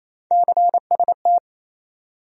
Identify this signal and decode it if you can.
Morse code, heard as a keyed tone, about 27 wpm, 700 Hz. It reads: CST
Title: USA Call Sign Postfixes - 3 Letters 28wpm